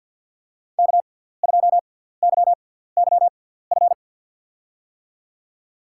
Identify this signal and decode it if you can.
Morse code, heard as a keyed tone, about 51 wpm, spaced as for 17 wpm, 700 Hz. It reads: K2YYF
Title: USA Call Signs - 5 Characters - 6x Character Spacing - 50wpm